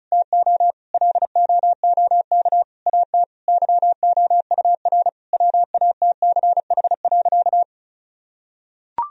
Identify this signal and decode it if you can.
Morse code, heard as a keyed tone, about 36 wpm, 700 Hz. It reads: TO LOOK AT YOUR WATCH.
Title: Sentences from Top 1000 Words - Review Part 1 - 35wpm